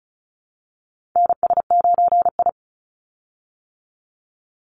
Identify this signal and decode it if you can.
Morse code, heard as a keyed tone, about 36 wpm, 700 Hz. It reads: NS9I
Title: Call Signs - 35wpm